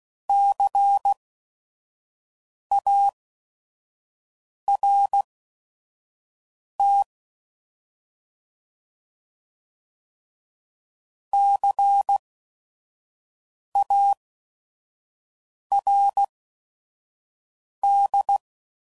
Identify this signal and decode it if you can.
Morse code, heard as a keyed tone, about 16 wpm, spaced as for 5 wpm, 800 Hz. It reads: CART CARD
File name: Prefixes / By First Letter / Cs